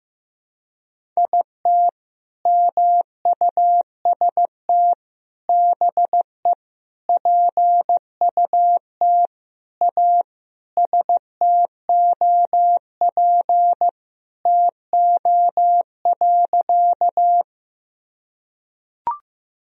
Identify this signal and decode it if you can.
Morse code, heard as a keyed tone, about 15 wpm, 700 Hz. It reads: IT MUST BE PUT A STOP TO.
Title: Sentences from Top 1000 Words - Review Part 1 - 15wpm